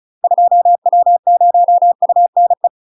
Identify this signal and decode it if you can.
Morse code, heard as a keyed tone, about 36 wpm, 700 Hz. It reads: 2W0UNE
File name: UK MASTER QSOs TRAINING / 35WPM / UK MASTER QSOs 0031